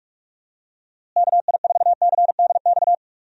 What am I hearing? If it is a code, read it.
Morse code, heard as a keyed tone, about 46 wpm, 700 Hz. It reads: KI4CDX
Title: USA Call Signs - 6 Characters - Repeated 2x - 45wpm